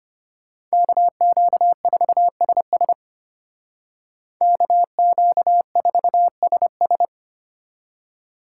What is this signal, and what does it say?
Morse code, heard as a keyed tone, about 33 wpm, 700 Hz. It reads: KQ4SS KQ4SS